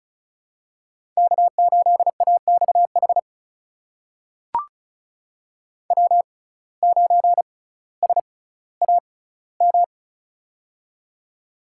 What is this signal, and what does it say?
Morse code, heard as a keyed tone, about 36 wpm, spaced as for 18 wpm, 700 Hz. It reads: K8AXH W9SAM
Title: USA Call Signs - 5 Characters - 6x Character Spacing - 35wpm